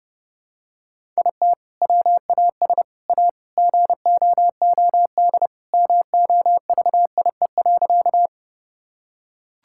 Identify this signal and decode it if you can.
Morse code, heard as a keyed tone, about 30 wpm, 700 Hz. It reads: IT WAS A GOOD MOVIE.
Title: Sentences from Top 1000 Words - Review Part 5 - 30wpm